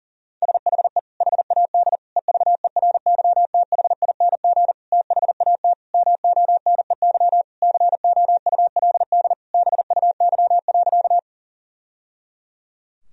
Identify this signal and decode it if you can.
Morse code, heard as a keyed tone, about 40 wpm, 700 Hz. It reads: SHE HAD EVERYTHING THAT MONEY COULD BUY.
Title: Sentences from Top 1000 Words - Review Part 4 - 40wpm